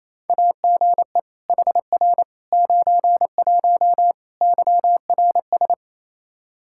Morse code, heard as a keyed tone, about 28 wpm, 700 Hz. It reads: AGE HR 91 YRS
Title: K1USN SST QSO Practice 28wpm